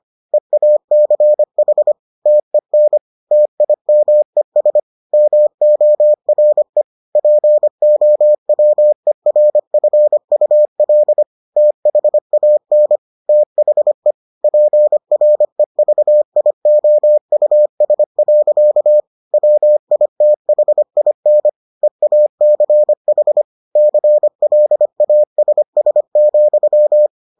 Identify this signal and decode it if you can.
Morse code, heard as a keyed tone, about 25 wpm, 600 Hz. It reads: EACH TEN TIMES MORE POWERFUL THAN THE PREVIOUS. WITHIN EACH CLASS,